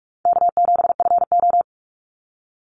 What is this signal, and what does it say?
Morse code, heard as a keyed tone, about 44 wpm, 700 Hz. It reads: K7FO
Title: NA SOTA Chaser Callsigns 45wpm